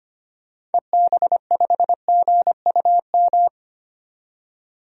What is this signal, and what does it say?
Morse code, heard as a keyed tone, about 25 wpm, 700 Hz. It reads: EB5GUM
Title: DXCC-100 Call Signs - 6 Characters - 25wpm